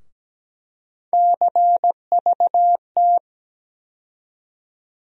Morse code, heard as a keyed tone, about 17 wpm, 700 Hz. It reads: CVT
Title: USA Call Sign Postfixes - 3 Letters 17wpm